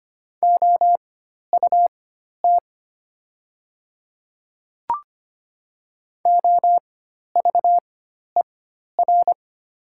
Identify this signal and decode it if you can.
Morse code, heard as a keyed tone, about 25 wpm, spaced as for 12 wpm, 700 Hz. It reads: OUT OVER
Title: 4x Character Spacing - Top 200 Words - 25wpm